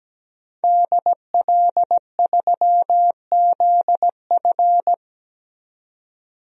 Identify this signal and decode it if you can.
Morse code, heard as a keyed tone, about 17 wpm, 700 Hz. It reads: DL3ZF
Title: DXCC-100 Call Signs - 5 Characters - 17wpm